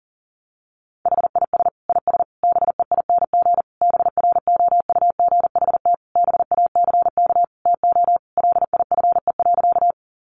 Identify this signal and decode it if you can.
Morse code, heard as a keyed tone, about 40 wpm, 700 Hz. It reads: HIS IS BEING BROUGHT BACK TO LIFE.
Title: Sentences from Top 600 Words 40wpm